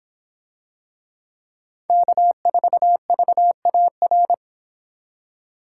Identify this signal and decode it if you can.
Morse code, heard as a keyed tone, about 24 wpm, 700 Hz. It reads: K4VAR